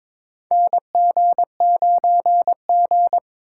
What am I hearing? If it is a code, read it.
Morse code, heard as a keyed tone, about 22 wpm, 700 Hz. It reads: NG9G
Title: USA Call Signs - 2x Character Spacing - 22wpm